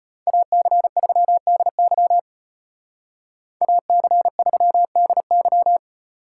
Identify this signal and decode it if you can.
Morse code, heard as a keyed tone, about 29 wpm, 700 Hz. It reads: AC3DY AC3DY
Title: Speed Racing - USA 2x2 Call Signs 25wpm